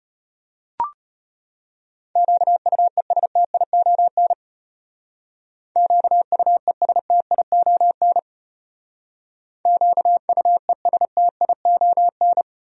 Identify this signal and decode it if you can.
Morse code, heard as a keyed tone, about 33 wpm, 700 Hz. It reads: QUESTION QUESTION QUESTION